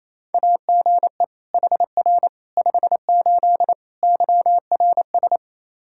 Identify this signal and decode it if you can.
Morse code, heard as a keyed tone, about 28 wpm, 700 Hz. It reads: AGE HR 58 YRS
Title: Practice QSO - Exchange 3 - More Info - 28wpm